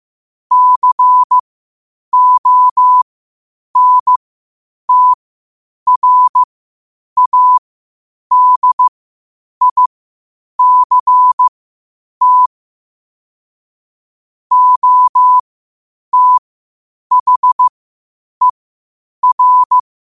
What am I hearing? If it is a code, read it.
Morse code, heard as a keyed tone, about 15 wpm, spaced as for 8 wpm, 1000 Hz. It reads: CONTRADICT OTHER
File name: qotdfiles / 8wpm